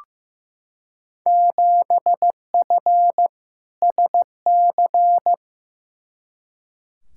Morse code, heard as a keyed tone, about 16 wpm, 700 Hz. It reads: 7F SC
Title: Field Day Exchanges Encore - 15wpm